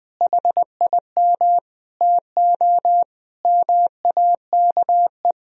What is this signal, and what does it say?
Morse code, heard as a keyed tone, about 21 wpm, 700 Hz. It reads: HIM TO MAKE
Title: Sentences from Top 1000 Words - Review Part 1 - 20wpm